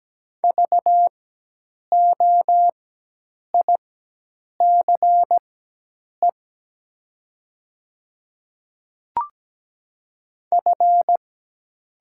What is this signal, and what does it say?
Morse code, heard as a keyed tone, about 17 wpm, spaced as for 8 wpm, 700 Hz. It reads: VOICE F
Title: 4x Character Spacing - Top 500 Words - 17wpm